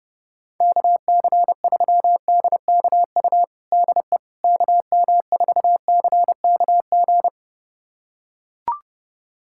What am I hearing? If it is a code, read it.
Morse code, heard as a keyed tone, about 30 wpm, 700 Hz. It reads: KC3DKU DE KM4CKG
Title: USA Callsign DE Callsign 30wpm